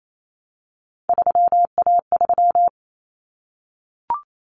Morse code, heard as a keyed tone, about 28 wpm, 700 Hz. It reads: 3A3
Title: Common Cut Numbers - 3-Digit 28wpm